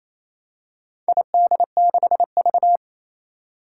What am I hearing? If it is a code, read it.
Morse code, heard as a keyed tone, about 28 wpm, 700 Hz. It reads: ID6V
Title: Four Challenging Letter-Number Combinations 28wpm